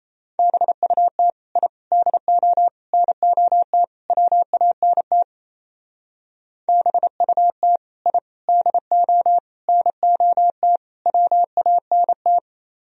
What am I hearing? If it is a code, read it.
Morse code, heard as a keyed tone, about 35 wpm, 700 Hz. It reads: BUT I DO NOT WANT BUT I DO NOT WANT